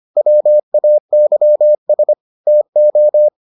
Morse code, heard as a keyed tone, about 25 wpm, 600 Hz. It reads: WAYS TO